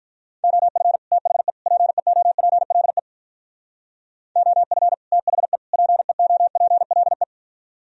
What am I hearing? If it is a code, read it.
Morse code, heard as a keyed tone, about 45 wpm, 700 Hz. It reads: OF THE PEOPLE OF THE PEOPLE